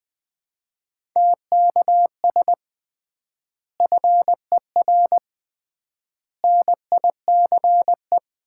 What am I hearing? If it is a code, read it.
Morse code, heard as a keyed tone, about 20 wpm, 700 Hz. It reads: TKS FER NICE
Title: Common QSO Phrases - 3x Word-Spacing - 20wpm